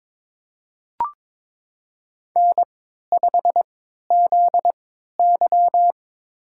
Morse code, heard as a keyed tone, about 22 wpm, spaced as for 13 wpm, 700 Hz. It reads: N5ZY
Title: USA Call Signs - 3x Character Spacing - 22wpm